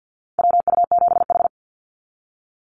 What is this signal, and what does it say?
Morse code, heard as a keyed tone, about 48 wpm, 700 Hz. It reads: WU7H
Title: NA SOTA Activator Callsigns 50wpm